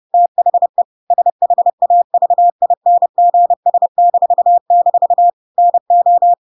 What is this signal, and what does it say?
Morse code, heard as a keyed tone, about 30 wpm, 700 Hz. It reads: THE SHAVINGS--NO
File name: Pinocchio / Pinocchio ch1 30 30 ik1hge